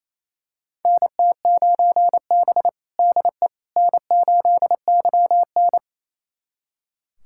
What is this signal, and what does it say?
Morse code, heard as a keyed tone, about 28 wpm, 700 Hz. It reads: NT9B DE N8YN
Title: USA Callsign DE Callsign - 4 Characters - 28wpm